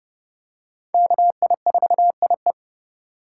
Morse code, heard as a keyed tone, about 30 wpm, 700 Hz. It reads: KI4IE
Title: USA Call Signs - 5 Characters - 4x Character Spacing - 30wpm